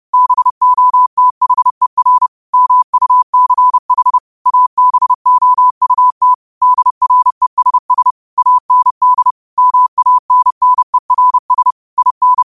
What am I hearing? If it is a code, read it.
Morse code, heard as a keyed tone, about 30 wpm, 1000 Hz. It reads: BOTHER MUCH ABOUT DRESS AND MANNERS IN
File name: qotdfiles / 30wpm